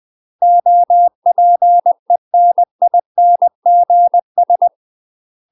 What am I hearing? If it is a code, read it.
Morse code, heard as a keyed tone, about 20 wpm, 700 Hz. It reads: OPENINGS
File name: UK MASTER QSOs TRAINING / 20WPM / UK MASTER QSOs 0012